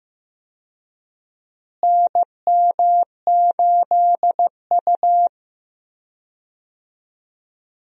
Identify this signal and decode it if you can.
Morse code, heard as a keyed tone, about 15 wpm, 700 Hz. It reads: NM8U
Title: USA Call Signs - 4 Characters - Repeated 2x - 15wpm